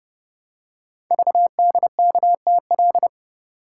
Morse code, heard as a keyed tone, about 28 wpm, 700 Hz. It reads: VDKTL
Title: Code Groups - 5 Letters - 30wpm